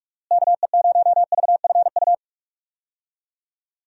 Morse code, heard as a keyed tone, about 44 wpm, 700 Hz. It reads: KE0VFU
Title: USA Call Signs - 6 Characters - Repeated 2x - 45wpm